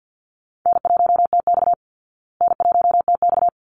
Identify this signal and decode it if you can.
Morse code, heard as a keyed tone, about 50 wpm, 700 Hz. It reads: N1TX N1TX
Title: Easy Call Signs Repeated 2x - 50wpm